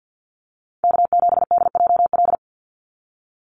Morse code, heard as a keyed tone, about 50 wpm, 700 Hz. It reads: K7DJL
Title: NA SOTA Chaser Callsigns 50wpm